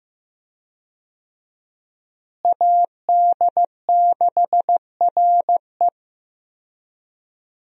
Morse code, heard as a keyed tone, about 15 wpm, 700 Hz. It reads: AD6RE
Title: USA Call Signs - 5 Characters - Repeated 3x - 15wpm